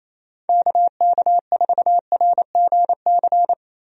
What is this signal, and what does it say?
Morse code, heard as a keyed tone, about 28 wpm, 700 Hz. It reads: KK4RGC
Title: USA Call Signs - 6 Characters - Repeated 3x - 28wpm